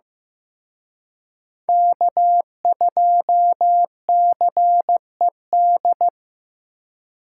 Morse code, heard as a keyed tone, about 18 wpm, 700 Hz. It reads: K2CED
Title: Speed Racing - USA Call Signs 15wpm